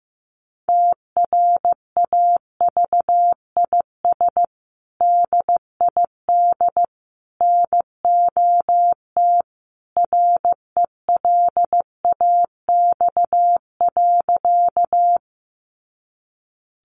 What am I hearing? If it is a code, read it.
Morse code, heard as a keyed tone, about 15 wpm, 700 Hz. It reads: TRAVIS DID NOT RELAX.
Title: A Sound Of Thunder 15wpm